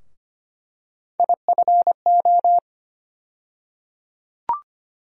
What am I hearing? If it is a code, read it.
Morse code, heard as a keyed tone, about 25 wpm, 700 Hz. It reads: IFO
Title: USA Call Sign Postfixes - 3 Letters 25wpm